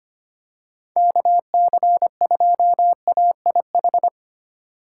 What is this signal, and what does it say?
Morse code, heard as a keyed tone, about 25 wpm, 700 Hz. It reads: KC2AIH
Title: USA Call Signs - 6 Characters - 3x Character Spacing - 25wpm